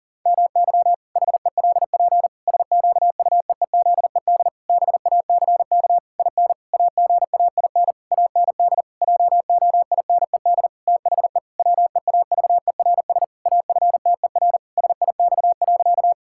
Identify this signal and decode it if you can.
Morse code, heard as a keyed tone, about 40 wpm, 700 Hz. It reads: MY HELP SQUEEZED BACK IN AGAIN AND JOINED THE WEAVERS AFTER SIX.
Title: Pangram Challenge v2 - 40wpm